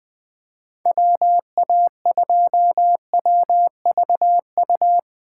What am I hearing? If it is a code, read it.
Morse code, heard as a keyed tone, about 23 wpm, 700 Hz. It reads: WA2WVU
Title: Speed Racing - USA Call Signs 20wpm